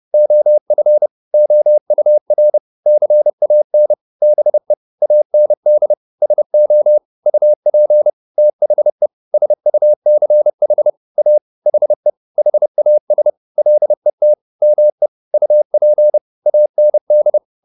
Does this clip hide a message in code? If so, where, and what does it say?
Yes: Morse code from 0.1 to 17.4 s, heard as a keyed tone, about 30 wpm, 600 Hz. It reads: OF OUR CAN BE AND SO UP THE SUCH A HE HAS LET ME UP AND